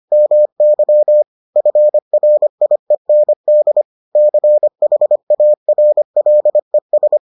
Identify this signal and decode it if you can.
Morse code, heard as a keyed tone, about 25 wpm, 600 Hz. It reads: MY FRIEND CHARLES